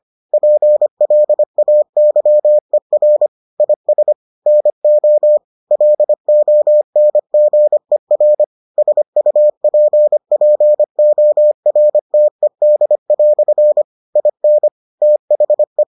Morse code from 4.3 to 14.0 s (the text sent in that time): NO LONGER SUPPORTED"